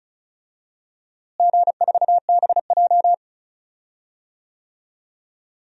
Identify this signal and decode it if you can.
Morse code, heard as a keyed tone, about 36 wpm, 700 Hz. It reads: G4BJ
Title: DXCC-100 Call Signs - 4 Characters - Repeated 2x - 35wpm